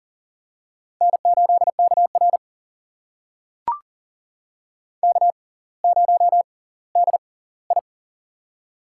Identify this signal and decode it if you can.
Morse code, heard as a keyed tone, about 40 wpm, spaced as for 20 wpm, 700 Hz. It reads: N8KR K0DI
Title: USA Call Signs - 4 Characters - 6x Character Spacing - 40wpm